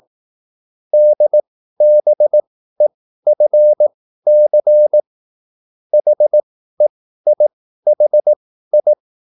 Morse code, heard as a keyed tone, about 18 wpm, spaced as for 13 wpm, 600 Hz. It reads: DBEFC HEIHI